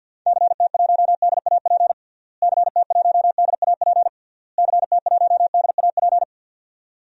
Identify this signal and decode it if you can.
Morse code, heard as a keyed tone, about 49 wpm, 700 Hz. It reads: CT1DAP CT1DAP CT1DAP
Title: DXCC-100 Call Signs - 6 Characters - Repeated 3x - 50wpm